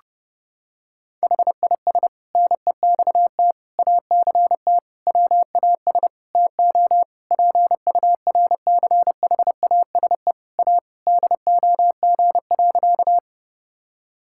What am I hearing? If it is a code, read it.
Morse code, heard as a keyed tone, about 30 wpm, 700 Hz. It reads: HIS NEXT ACT WAS TO PURCHASE A DOG.
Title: Sentences from Top 1000 Words - Review Part 5 - 30wpm